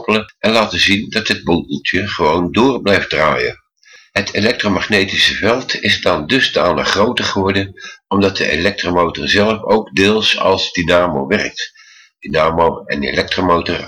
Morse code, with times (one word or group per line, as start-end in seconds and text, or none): none